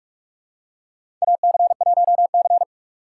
Morse code, heard as a keyed tone, about 46 wpm, 700 Hz. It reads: AC1C
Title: USA Call Signs - 4 Characters - 4x Character Spacing - 45wpm